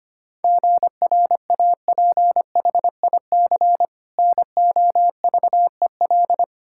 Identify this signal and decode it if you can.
Morse code, heard as a keyed tone, about 25 wpm, 700 Hz. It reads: GRAPHIC NOVEL
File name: practice / 25wpm / Difficult Words with Hints - 25wpm